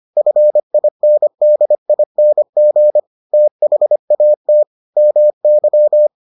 Morse code, heard as a keyed tone, about 25 wpm, 600 Hz. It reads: FINDING THAT MY